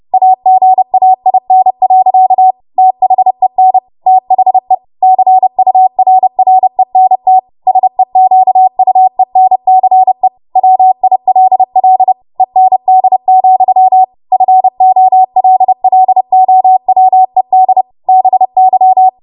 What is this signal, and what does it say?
Morse code, heard as a keyed tone, about 30 wpm, 750 Hz. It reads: AGAIN. THEN THE CURRENT SEQUENCE WILL END, FOLLOWED BY
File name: Morse / 30WPM